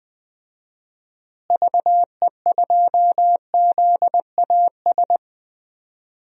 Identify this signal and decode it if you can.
Morse code, heard as a keyed tone, about 20 wpm, 700 Hz. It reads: VE2ZAS
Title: DXCC-100 Call Signs - 6 Characters - Repeated 2x - 20wpm